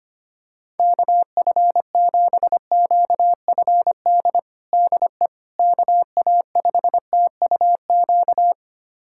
Morse code, heard as a keyed tone, about 25 wpm, 700 Hz. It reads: KF7QFD DE KA5TUQ
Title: USA Callsign DE Callsign - 6 Characters - 25wpm